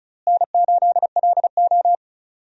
Morse code, heard as a keyed tone, about 36 wpm, 700 Hz. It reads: N8LO
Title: USA Call Signs - 4 Characters - Repeated 2x - 35wpm